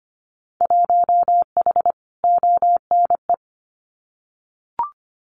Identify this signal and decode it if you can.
Morse code, heard as a keyed tone, about 25 wpm, 700 Hz. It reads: 1H ONE
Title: Winter Field Day Exchanges 25wpm